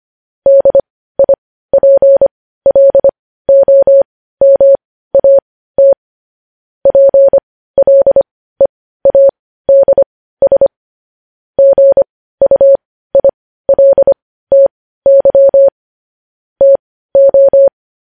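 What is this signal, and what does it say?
Morse code, heard as a keyed tone, about 25 wpm, spaced as for 15 wpm, 550 Hz. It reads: DIPLOMAT PLEADS GUILTY TO
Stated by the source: Morse code news headlines at 15 wpm.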